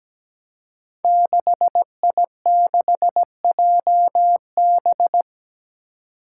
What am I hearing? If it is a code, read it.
Morse code, heard as a keyed tone, about 17 wpm, 700 Hz. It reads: 6I6JB
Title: Five Challenging Letter-Number Combinations 17wpm